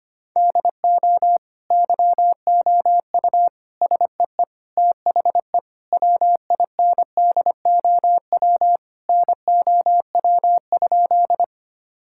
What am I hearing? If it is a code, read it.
Morse code, heard as a keyed tone, about 25 wpm, 700 Hz. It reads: DO YOU SEE THE WINDOW NOW?
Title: Sentences from Top 1600 Words 25wpm